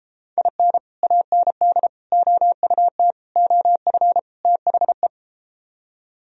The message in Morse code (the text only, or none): IN AND OUT OF THE